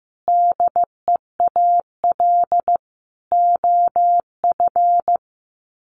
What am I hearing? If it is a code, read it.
Morse code, heard as a keyed tone, about 15 wpm, 700 Hz. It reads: DEAL OF
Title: Sets of 4 Words 15wpm